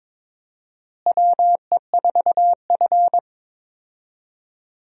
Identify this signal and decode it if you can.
Morse code, heard as a keyed tone, about 22 wpm, 700 Hz. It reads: WE4F
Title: USA Call Signs - 4 Characters - Repeated 2x - 22wpm